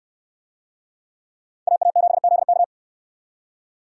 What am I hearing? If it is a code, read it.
Morse code, heard as a keyed tone, about 68 wpm, 700 Hz. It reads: AA7ZK